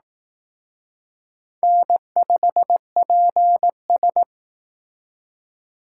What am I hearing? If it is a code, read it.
Morse code, heard as a keyed tone, about 18 wpm, 700 Hz. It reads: N5PS
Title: Speed Racing - USA 1x2 Call Signs 15wpm